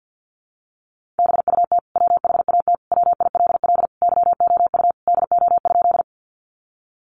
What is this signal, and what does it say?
Morse code, heard as a keyed tone, about 50 wpm, 700 Hz. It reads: BUT WHAT WILL YOU DO?
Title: Sentences from Top 100 Words 50wpm